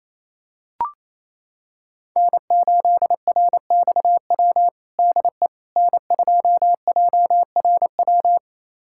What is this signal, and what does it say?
Morse code, heard as a keyed tone, about 28 wpm, 700 Hz. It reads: N8RXW DE N2JRW
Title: USA Callsign DE Callsign - 5 Characters - 28wpm